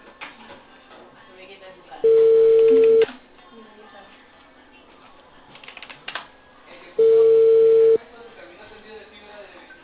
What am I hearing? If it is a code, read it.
Ringback tone (China, 450 Hz), with speech.